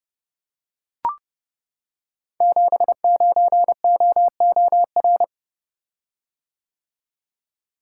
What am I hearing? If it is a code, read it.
Morse code, heard as a keyed tone, about 28 wpm, 700 Hz. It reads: 79OOR
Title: Code Groups - 5 Letter:Numbers - 30wpm